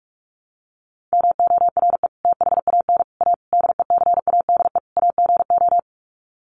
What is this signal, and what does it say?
Morse code, heard as a keyed tone, about 44 wpm, 700 Hz. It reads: MORE THAN A DECADE AGO
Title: Sets of 5 Words Encore2 45wpm